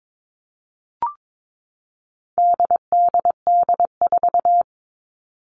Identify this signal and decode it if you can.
Morse code, heard as a keyed tone, about 22 wpm, 700 Hz. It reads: DDD4
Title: All Cut Numbers - 4-Digit 22wpm